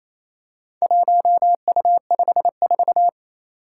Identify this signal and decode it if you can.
Morse code, heard as a keyed tone, about 28 wpm, 700 Hz. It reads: 1U54
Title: Four Challenging Letter-Number Combinations 28wpm